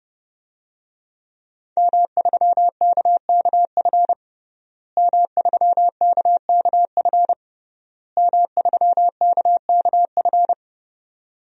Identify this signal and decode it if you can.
Morse code, heard as a keyed tone, about 30 wpm, 700 Hz. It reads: M3KKF M3KKF M3KKF
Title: DXCC-100 Call Signs - 5 Characters - Repeated 3x - 30wpm